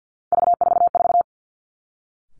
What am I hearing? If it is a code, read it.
Morse code, heard as a keyed tone, about 50 wpm, 700 Hz. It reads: V44
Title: Three Challenging Letter-Number Combinations 50wpm